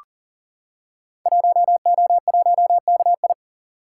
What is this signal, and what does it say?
Morse code, heard as a keyed tone, about 40 wpm, 700 Hz. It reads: 1O1KI
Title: Code Groups - 5 Letter:Numbers - 40wpm